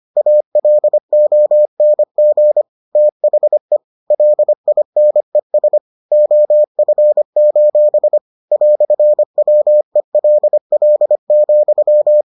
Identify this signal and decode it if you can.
Morse code, heard as a keyed tone, about 25 wpm, 600 Hz. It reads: ALONG THE LINES OF: "WELL,